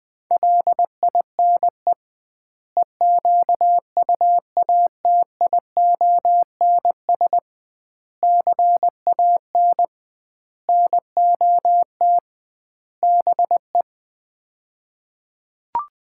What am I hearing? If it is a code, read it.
Morse code, heard as a keyed tone, about 20 wpm, 700 Hz. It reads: LINE EQUATIONS CAN NOT BE
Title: 2x Word-Spacing - Sets of 5 Words - 20wpm